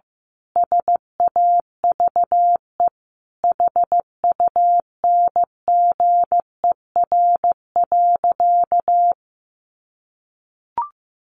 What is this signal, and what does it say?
Morse code, heard as a keyed tone, about 15 wpm, 700 Hz. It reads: SAVE HUNGER.
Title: A Sound Of Thunder 15wpm